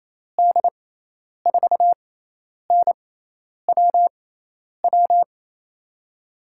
Morse code, heard as a keyed tone, about 28 wpm, spaced as for 10 wpm, 700 Hz. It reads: D4NWW
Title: USA Call Signs - 6 Characters - 6x Character Spacing - 28wpm